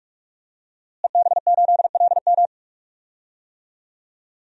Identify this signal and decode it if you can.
Morse code, heard as a keyed tone, about 43 wpm, 700 Hz. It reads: EB8LM
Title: DXCC-100 Call Signs - 5 Characters - Repeated 3x - 45wpm